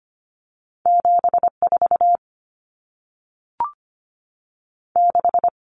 Morse code, heard as a keyed tone, about 25 wpm, 700 Hz. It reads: 74 6
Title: Numbers - 2-Digit 25wpm